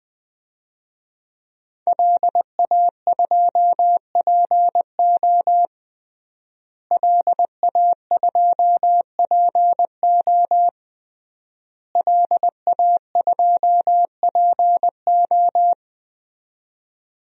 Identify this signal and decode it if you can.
Morse code, heard as a keyed tone, about 21 wpm, 700 Hz. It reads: LA2PO LA2PO LA2PO
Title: DXCC-100 Call Signs - 5 Characters - Repeated 3x - 20wpm